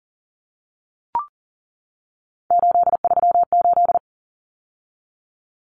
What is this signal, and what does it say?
Morse code, heard as a keyed tone, about 40 wpm, 700 Hz. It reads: 838
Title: Numbers - 3-Digit 40wpm